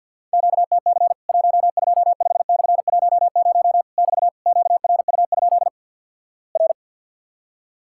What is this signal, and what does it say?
Morse code, heard as a keyed tone, about 50 wpm, 700 Hz. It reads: QTC 125/10 <BT> QRU? R
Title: Worked All Europe DX Contest - QTC Exchange - 50wpm